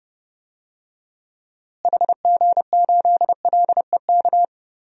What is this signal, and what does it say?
Morse code, heard as a keyed tone, about 30 wpm, 700 Hz. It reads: HG8LEK
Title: DXCC-100 Call Signs - 6 Characters - Repeated 2x - 30wpm